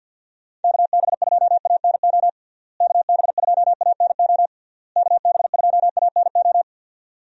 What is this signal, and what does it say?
Morse code, heard as a keyed tone, about 51 wpm, 700 Hz. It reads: KB2ANO KB2ANO KB2ANO
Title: USA Call Signs - 6 Characters - Repeated 3x - 50wpm